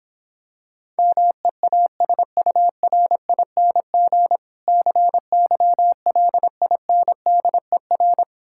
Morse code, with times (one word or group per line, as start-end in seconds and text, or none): 1.0-4.4 MEASURING
4.7-8.2 CYLINDER